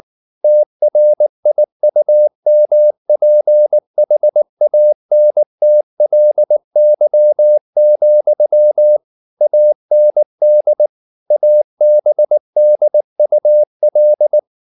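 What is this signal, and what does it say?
Morse code, heard as a keyed tone, about 19 wpm, 600 Hz. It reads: TRIUMPHANTLY, AND ABDUL
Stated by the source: Morse code at 19 wpm from CW med Gnister, reading Ken Follett's novel Never in English.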